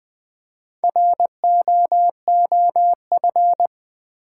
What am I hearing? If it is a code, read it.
Morse code, heard as a keyed tone, about 20 wpm, 700 Hz. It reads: ROOF
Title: Four Letter Words - 20wpm